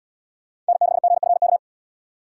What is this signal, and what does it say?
Morse code, heard as a keyed tone, about 68 wpm, 700 Hz. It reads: N6GFK